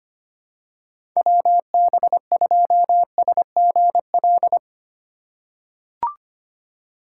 Morse code, heard as a keyed tone, about 25 wpm, 700 Hz. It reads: WB2SGL
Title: USA Call Signs - 6 Characters - 2x Character Spacing - 25wpm